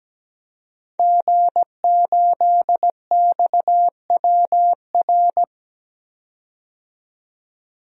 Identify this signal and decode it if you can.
Morse code, heard as a keyed tone, about 17 wpm, 700 Hz. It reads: G8XWR
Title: DXCC-100 Call Signs - 5 Characters - 17wpm